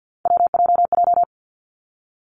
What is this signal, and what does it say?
Morse code, heard as a keyed tone, about 50 wpm, 700 Hz. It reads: WJJ
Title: Three Challenging Letter-Number Combinations 50wpm